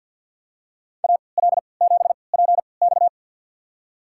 Morse code, heard as a keyed tone, about 52 wpm, spaced as for 28 wpm, 700 Hz. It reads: AL7PX
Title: USA Call Signs - 5 Characters - 3x Character Spacing - 50wpm